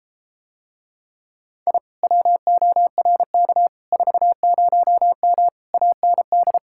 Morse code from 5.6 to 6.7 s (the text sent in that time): AND